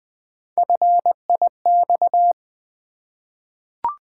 Morse code, heard as a keyed tone, about 20 wpm, 700 Hz. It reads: FIX
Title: Three Letter Words - 20wpm